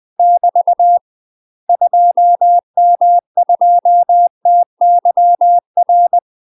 Morse code, heard as a keyed tone, about 20 wpm, 700 Hz. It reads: <BT> 2M2TYR
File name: UK MASTER QSOs TRAINING / 20WPM / UK MASTER QSOs 0003